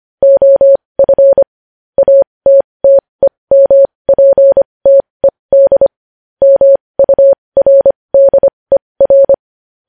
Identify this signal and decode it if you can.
Morse code, heard as a keyed tone, about 25 wpm, spaced as for 20 wpm, 550 Hz. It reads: OF ATTEMPTED MURDER
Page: News Headlines in Morse Code at 20 WPM